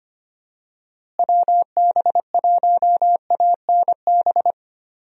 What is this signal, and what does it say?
Morse code, heard as a keyed tone, about 25 wpm, 700 Hz. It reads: WB1ANB
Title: USA Call Signs - 6 Characters - Repeated 3x - 25wpm